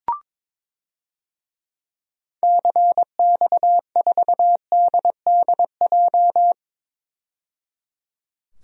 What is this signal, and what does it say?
Morse code, heard as a keyed tone, about 22 wpm, 700 Hz. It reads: CX4DDJ
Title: DXCC-100 Call Signs - 6 Characters - 22wpm